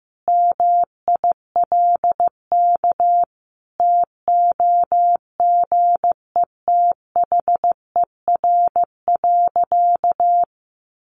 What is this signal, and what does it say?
Morse code, heard as a keyed tone, about 15 wpm, 700 Hz. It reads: MILK TOGETHER.
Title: Sentences from Top 1500 Words 15wpm